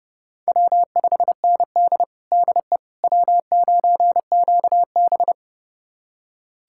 Morse code, heard as a keyed tone, about 30 wpm, 700 Hz. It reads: W5ND DE W9QB
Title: USA Callsign DE Callsign - 4 Characters - 30wpm